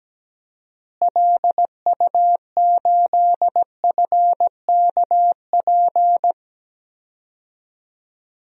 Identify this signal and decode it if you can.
Morse code, heard as a keyed tone, about 17 wpm, 700 Hz. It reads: LU8FKP
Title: DXCC-100 Call Signs - 6 Characters - Repeated 3x - 17wpm